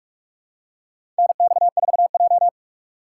Morse code, heard as a keyed tone, about 44 wpm, 700 Hz. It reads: NX4J
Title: USA Call Signs - 4 Characters - 2x Character Spacing - 45wpm